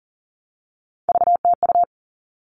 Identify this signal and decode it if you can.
Morse code, heard as a keyed tone, about 40 wpm, 700 Hz. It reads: VTU
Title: All Cut Numbers - 3-Digit 40wpm